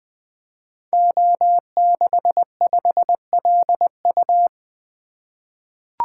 Morse code, heard as a keyed tone, about 20 wpm, 700 Hz. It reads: O65LU
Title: Code Groups - 5 Letter:Numbers - 20wpm